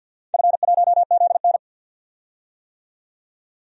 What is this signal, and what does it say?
Morse code, heard as a keyed tone, about 46 wpm, 700 Hz. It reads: F1ZN
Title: DXCC-100 Call Signs - 4 Characters - 50wpm